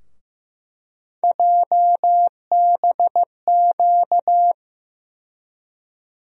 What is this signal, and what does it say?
Morse code, heard as a keyed tone, about 15 wpm, 700 Hz. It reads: JBQ
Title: USA Call Sign Postfixes - 3 Letters 15wpm